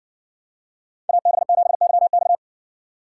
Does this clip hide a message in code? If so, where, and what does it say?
Yes: Morse code from 1.1 to 2.4 s, heard as a keyed tone, about 59 wpm, 700 Hz. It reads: AB7YX